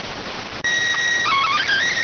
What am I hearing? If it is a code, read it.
SSTV (slow-scan television), PD 290.